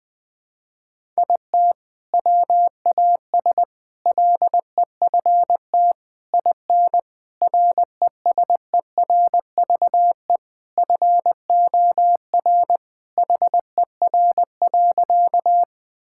Morse code, heard as a keyed tone, about 20 wpm, 700 Hz. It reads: IT WAS LEFT IN RESERVE FOR HER.
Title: Sentences from Top 1000 Words - Review Part 6 - 20wpm